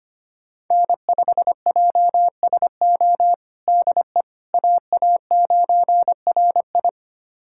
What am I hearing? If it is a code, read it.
Morse code, heard as a keyed tone, about 25 wpm, 700 Hz. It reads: N5JSO DE AA9RI
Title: USA Callsign DE Callsign - 5 Characters - 25wpm